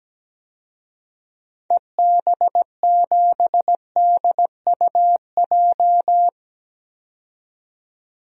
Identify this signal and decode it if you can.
Morse code, heard as a keyed tone, about 17 wpm, 700 Hz. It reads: EB7DUJ
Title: DXCC-100 Call Signs - 6 Characters - 17wpm